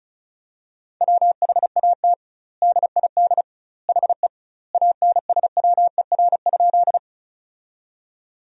Morse code, heard as a keyed tone, about 36 wpm, 700 Hz. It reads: WHAT DID HE ANSWER?
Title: 2x Word-Spacing - Sentences from Top 300 Words - 35wpm